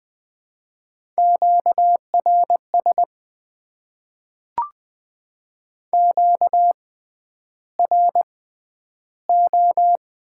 Morse code, heard as a keyed tone, about 20 wpm, spaced as for 10 wpm, 700 Hz. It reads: QRS QRO
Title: Top 90 QSO Elements - 6x Character Spacing - 20wpm